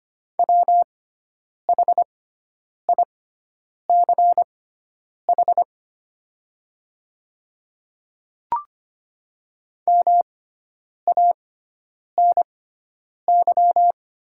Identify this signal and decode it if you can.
Morse code, heard as a keyed tone, about 25 wpm, spaced as for 9 wpm, 700 Hz. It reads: WHICH MANY
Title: 6x Character Spacing - Top 200 Words - 25wpm